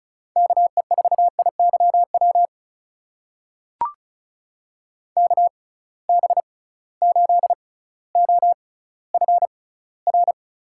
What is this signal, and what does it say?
Morse code, heard as a keyed tone, about 36 wpm, spaced as for 18 wpm, 700 Hz. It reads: KE4IYW KB8OFR
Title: USA Call Signs - 6 Characters - 6x Character Spacing - 35wpm